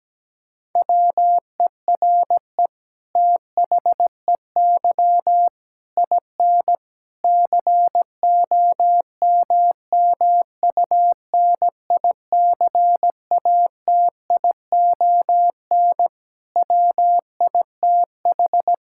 Morse code, heard as a keyed tone, about 17 wpm, 700 Hz. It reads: WERE THEY IN COMMUNICATION WITH
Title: Sentences from Top 1000 Words - Review Part 6 - 17wpm